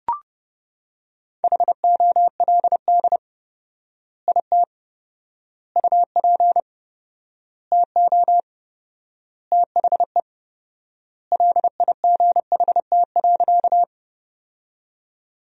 Morse code, heard as a keyed tone, about 30 wpm, 700 Hz. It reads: HOLD IT UP TO THE LIGHT.
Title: 4x Word-Spacing - Sentences from Top 400 Words - 30wpm